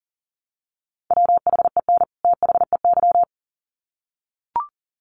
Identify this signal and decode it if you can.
Morse code, heard as a keyed tone, about 40 wpm, 700 Hz. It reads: WHEN THEY
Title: Sets of 2 Words 40wpm